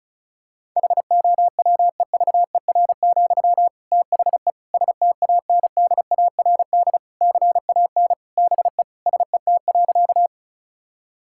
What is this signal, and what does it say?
Morse code, heard as a keyed tone, about 36 wpm, 700 Hz. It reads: HOWEVER, THE STANDARD CAN BE SET.
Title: Sentences from Top 1000 Words - Review Part 6 - 35wpm